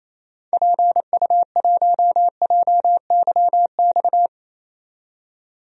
Morse code, heard as a keyed tone, about 28 wpm, 700 Hz. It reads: PU1JYX
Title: DXCC-100 Call Signs - 6 Characters - 28wpm